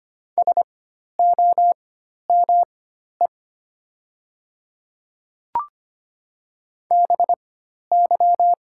Morse code, heard as a keyed tone, about 25 wpm, spaced as for 12 wpm, 700 Hz. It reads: SOME BY
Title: 4x Character Spacing - Top 100 Words - 25wpm